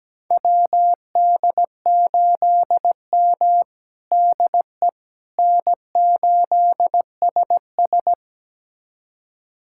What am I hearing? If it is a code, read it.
Morse code, heard as a keyed tone, about 17 wpm, 700 Hz. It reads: WD8M DE N8SS
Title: USA Callsign DE Callsign - 4 Characters - 17wpm